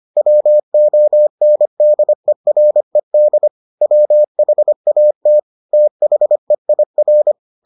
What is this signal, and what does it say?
Morse code, heard as a keyed tone, about 25 wpm, 600 Hz. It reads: WONDERED WHAT THEIR